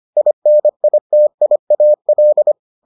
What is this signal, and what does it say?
Morse code, heard as a keyed tone, about 25 wpm, 600 Hz. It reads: INITIAL